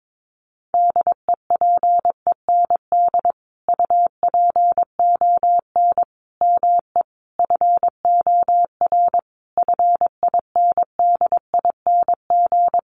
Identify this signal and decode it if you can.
Morse code, heard as a keyed tone, about 22 wpm, 700 Hz. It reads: DEPEND UPON ME FOR FINDING
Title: Sentences from Top 1300 Words 22wpm